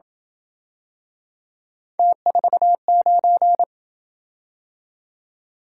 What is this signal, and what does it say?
Morse code, heard as a keyed tone, about 26 wpm, 700 Hz. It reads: T49